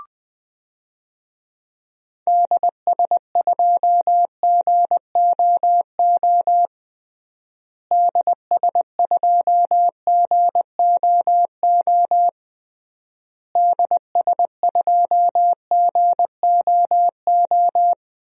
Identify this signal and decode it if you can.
Morse code, heard as a keyed tone, about 21 wpm, 700 Hz. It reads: DS2GOO DS2GOO DS2GOO
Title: DXCC-100 Call Signs - 6 Characters - Repeated 3x - 20wpm